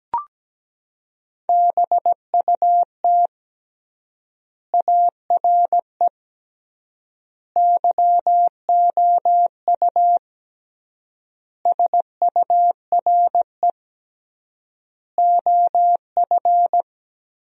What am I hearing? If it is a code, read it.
Morse code, heard as a keyed tone, about 17 wpm, 700 Hz. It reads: BUT ARE YOU SURE OF
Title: 3x Word-Spacing - Sentences from Top 300 Words - 17wpm